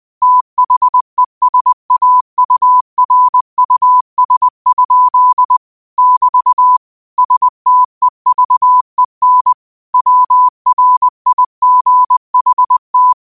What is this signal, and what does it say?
Morse code, heard as a keyed tone, about 20 wpm, 1000 Hz. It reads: THESAURUS? <BT> STEVEN WRIGHT
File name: qotdfiles / 20wpm